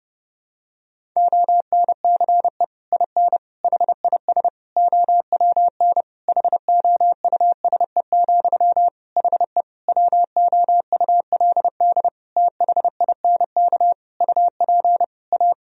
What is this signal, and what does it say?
Morse code, heard as a keyed tone, about 30 wpm, 700 Hz. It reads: ONCE IN HIS OWN HOUSE, HE WOULD THINK UP A
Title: Sentences from Top 1000 Words - Review Part 2 - 30wpm